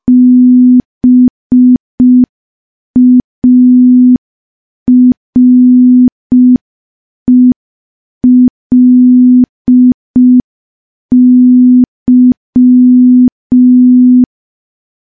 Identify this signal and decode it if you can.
Morse code, heard as a keyed tone, about 5 wpm, 250 Hz. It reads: BARELY
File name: morse / morse-training-05WPM